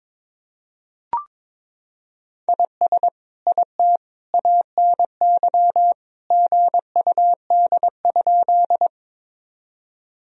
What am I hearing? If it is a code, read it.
Morse code, heard as a keyed tone, about 22 wpm, 700 Hz. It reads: IS IT ANY GUD?
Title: Sentences from Top 200 Words and Abbreviations 22wpm